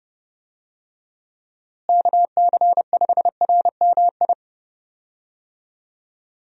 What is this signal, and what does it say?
Morse code, heard as a keyed tone, about 30 wpm, 700 Hz. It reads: KC5RMI
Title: USA Call Signs - 6 Characters - Repeated 3x - 30wpm